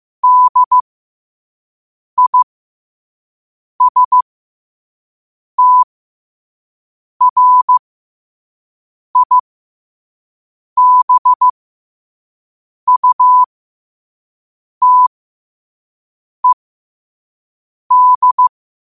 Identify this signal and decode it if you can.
Morse code, heard as a keyed tone, about 15 wpm, spaced as for 5 wpm, 1000 Hz. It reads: DISTRIBUTED
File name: qotdfiles / 5wpm